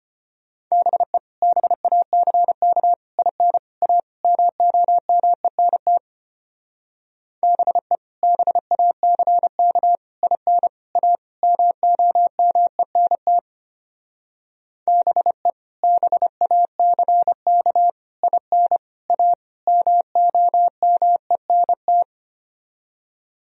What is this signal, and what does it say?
Morse code, heard as a keyed tone, about 31 wpm, 700 Hz. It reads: BE BACK IN A MOMENT BE BACK IN A MOMENT BE BACK IN A MOMENT
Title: Speed-Racing - Sets of 5 Words 25wpm